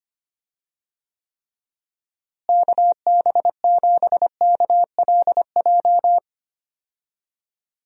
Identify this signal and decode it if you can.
Morse code, heard as a keyed tone, about 25 wpm, 700 Hz. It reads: KB7KLJ
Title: USA Call Signs - 6 Characters - Repeated 2x - 25wpm